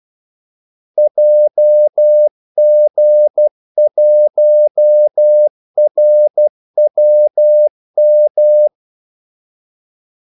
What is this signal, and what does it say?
Morse code, heard as a keyed tone, about 12 wpm, 600 Hz. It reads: JG1RWM
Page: 2 ekstra ordmellemrum mellem kaldesignalerne